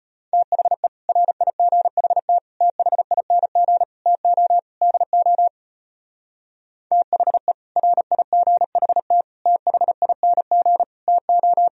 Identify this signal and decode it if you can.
Morse code, heard as a keyed tone, about 30 wpm, 700 Hz. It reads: THE RIGHT THING TO DO THE RIGHT THING TO
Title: Speed-Racing - Sets of 5 Words 25wpm